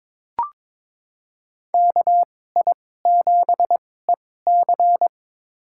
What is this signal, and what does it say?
Morse code, heard as a keyed tone, about 21 wpm, spaced as for 16 wpm, 700 Hz. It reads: KI7EC
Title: USA Call Signs - 5 Characters - 2x Character Spacing - 22wpm